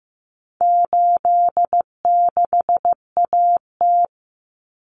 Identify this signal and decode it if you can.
Morse code, heard as a keyed tone, about 15 wpm, 700 Hz. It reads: 86AT
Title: Common Cut Numbers - 4-Digit 15wpm